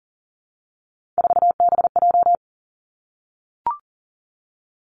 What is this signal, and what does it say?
Morse code, heard as a keyed tone, about 38 wpm, 690 Hz. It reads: 4BJ
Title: Three Challenging Letter-Number Combinations 40wpm